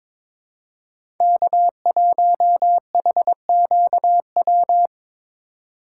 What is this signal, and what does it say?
Morse code, heard as a keyed tone, about 23 wpm, 700 Hz. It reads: K1HQW
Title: USA Call Signs - 5 Characters - 3x Character Spacing - 22wpm